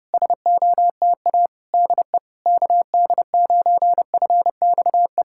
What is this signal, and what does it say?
Morse code, heard as a keyed tone, about 30 wpm, 690 Hz. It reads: SOTA DE KD9FXE
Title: SOTA QSO Practice 30wpm